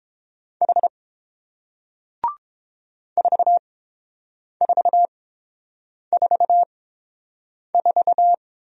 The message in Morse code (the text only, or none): H 4444